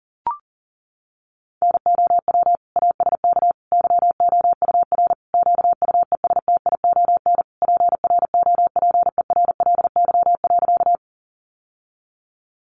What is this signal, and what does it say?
Morse code, heard as a keyed tone, about 40 wpm, 700 Hz. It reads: NOW ASK YOUR QUESTION PROPERLY.
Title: Sentences from Top 1100 Words 40wpm